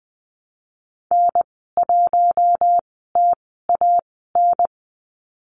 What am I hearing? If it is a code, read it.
Morse code, heard as a keyed tone, about 20 wpm, spaced as for 14 wpm, 700 Hz. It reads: N1TAN
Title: 042 Lesson 8g - Number 1 - Call Signs 20wpm